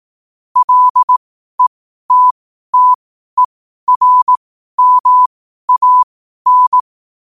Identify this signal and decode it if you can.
Morse code, heard as a keyed tone, about 18 wpm, spaced as for 12 wpm, 1000 Hz. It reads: LETTERMAN
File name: qotdfiles / 12wpm